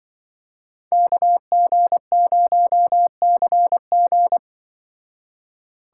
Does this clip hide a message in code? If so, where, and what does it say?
Yes: Morse code from 0.9 to 4.4 s, heard as a keyed tone, about 25 wpm, 700 Hz. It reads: KG0CG